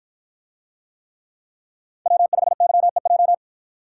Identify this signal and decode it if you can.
Morse code, heard as a keyed tone, about 50 wpm, 700 Hz. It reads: W5YEJ